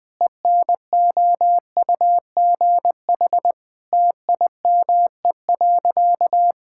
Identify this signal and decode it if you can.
Morse code, heard as a keyed tone, about 21 wpm, 700 Hz. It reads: ENOUGH TIME.
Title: Sentences with 4-Word Phrases - 20wpm